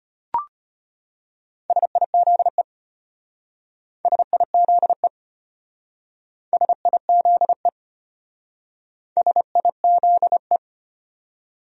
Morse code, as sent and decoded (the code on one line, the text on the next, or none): ... .. --.. / ... .. --.. / ... .. --.. / ... .. --.. .
SIZE SIZE SIZE SIZE